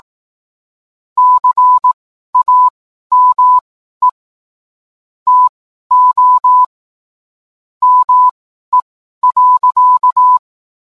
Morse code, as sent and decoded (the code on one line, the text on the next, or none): -.-. .- -- . / - --- / -- . .-.-.-
CAME TO ME.